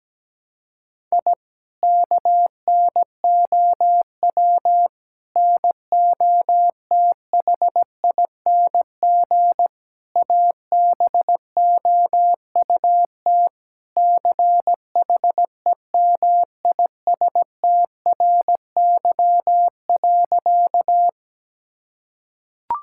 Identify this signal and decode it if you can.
Morse code, heard as a keyed tone, about 17 wpm, 700 Hz. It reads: I KNOW NOTHING ABOUT CHEMISTRY.
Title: Sentences from Top 1000 Words - Review Part 6 - 17wpm